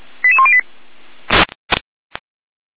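Five-tone selective call (ZVEI), heard as quick runs of five tones, tones of about 70 ms. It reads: 88108
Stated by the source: Received on an SDR; ZVEI five-tone call.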